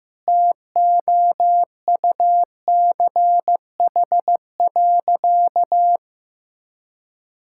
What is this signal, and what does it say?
Morse code, heard as a keyed tone, about 15 wpm, 700 Hz. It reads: TOUCH.
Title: Sentences from Top 1600 Words 15wpm